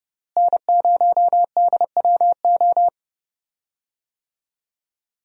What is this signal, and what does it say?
Morse code, heard as a keyed tone, about 31 wpm, 700 Hz. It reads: N0DWO
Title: USA Call Signs - 5 Characters - Repeated 2x - 30wpm